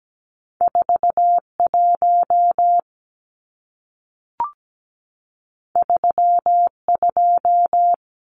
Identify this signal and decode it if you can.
Morse code, heard as a keyed tone, about 17 wpm, 700 Hz. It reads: 41 32
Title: Numbers - 2-Digit 17wpm